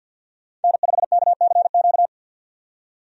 Morse code, heard as a keyed tone, about 51 wpm, 700 Hz. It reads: N5KCQ
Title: USA Call Signs - 5 Characters - 50wpm